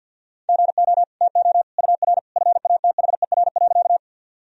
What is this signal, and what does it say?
Morse code, heard as a keyed tone, about 52 wpm, 700 Hz. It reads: GO TO UR FATHER.
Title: Sentences from Top 300 Words and Abbreviations 50wpm